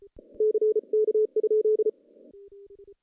Morse code, heard as a keyed tone, about 33 wpm, 430 Hz. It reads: CK?
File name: contests / CW-SS-k5tr- -0100